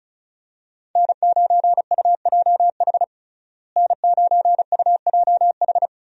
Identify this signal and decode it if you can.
Morse code, heard as a keyed tone, about 36 wpm, 700 Hz. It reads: N9UJH N9UJH
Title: USA Call Signs - 5 Characters - Repeated 3x - 35wpm